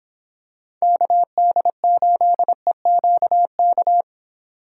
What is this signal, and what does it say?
Morse code, heard as a keyed tone, about 25 wpm, 700 Hz. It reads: KD8EQK